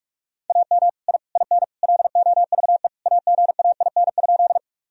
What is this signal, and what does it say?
Morse code, heard as a keyed tone, about 43 wpm, 700 Hz. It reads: AM I IN LOVE AGAIN?
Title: Sentences from Top 1000 Words - Review Part 2 - 45wpm